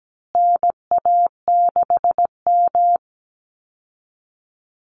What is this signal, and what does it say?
Morse code, heard as a keyed tone, about 17 wpm, 700 Hz. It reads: NA6M
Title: Call Signs Repeated 3x - 17wpm